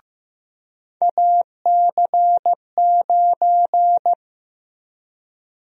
Morse code, heard as a keyed tone, about 15 wpm, 700 Hz. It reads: AC9
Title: USA Call Sign Prefixes - Repeated 2x - 15wpm